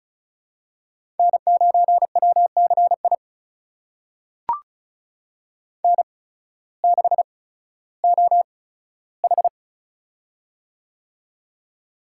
Morse code, heard as a keyed tone, about 36 wpm, spaced as for 16 wpm, 700 Hz. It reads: N9WCI N6OH
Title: USA Call Signs - 8x Character Spacing - 35wpm